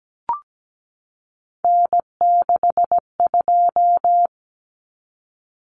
Morse code, heard as a keyed tone, about 17 wpm, 700 Hz. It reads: N62
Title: Common Cut Numbers - 3-Digit 17wpm